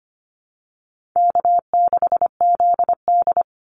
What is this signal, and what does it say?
Morse code, heard as a keyed tone, about 25 wpm, 700 Hz. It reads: K6ZD
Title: Call Signs Repeated 2x - 25wpm